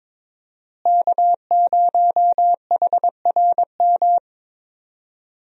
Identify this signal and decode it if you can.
Morse code, heard as a keyed tone, about 22 wpm, 700 Hz. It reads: K0HRM
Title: USA Call Signs - 5 Characters - Repeated 2x - 22wpm